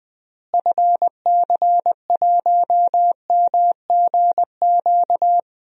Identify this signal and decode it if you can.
Morse code, heard as a keyed tone, about 20 wpm, 700 Hz. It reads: FC1MGQ
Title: DXCC-100 Call Signs - 6 Characters - Repeated 2x - 20wpm